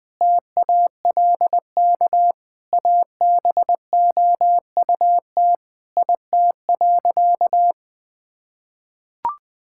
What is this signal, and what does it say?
Morse code, heard as a keyed tone, about 21 wpm, 700 Hz. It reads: TALK ABOUT IT.
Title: Sentences with 5-Word Phrases - 20wpm